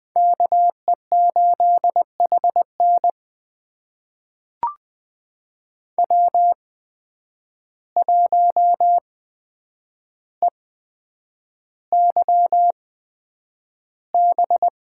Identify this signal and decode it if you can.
Morse code, heard as a keyed tone, about 20 wpm, spaced as for 9 wpm, 700 Hz. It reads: KE8HN W1EYB
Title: USA Call Signs - 5 Characters - 8x Character Spacing - 20wpm